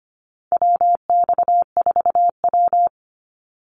Morse code, heard as a keyed tone, about 25 wpm, 700 Hz. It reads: WX4W
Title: Call Signs - 25wpm